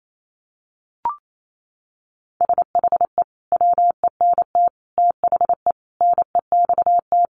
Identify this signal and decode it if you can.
Morse code, heard as a keyed tone, about 28 wpm, 700 Hz. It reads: SHE WENT THE NEXT
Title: Sentences from Top 1500 Words 28wpm